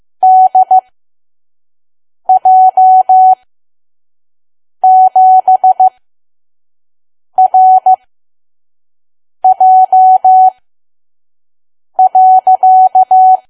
Morse code, heard as a keyed tone, about 15 wpm, spaced as for 5 wpm, 750 Hz. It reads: DJ7RJ.
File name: Archive / 5 WPM files / PM